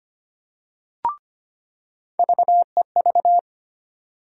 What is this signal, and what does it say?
Morse code, heard as a keyed tone, about 25 wpm, 700 Hz. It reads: VEV
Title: USA Call Sign Postfixes - 3 Letters 25wpm